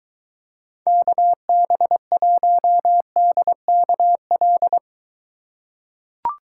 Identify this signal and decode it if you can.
Morse code, heard as a keyed tone, about 21 wpm, 700 Hz. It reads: KB1DKL